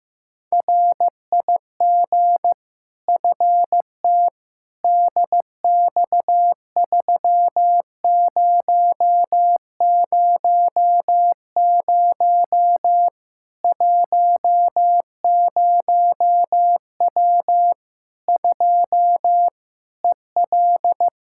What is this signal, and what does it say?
Morse code, heard as a keyed tone, about 15 wpm, 700 Hz. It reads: RIG FT DX3000 10W 2 EL
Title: Practice QSO - Exchange 2 - Info - 15wpm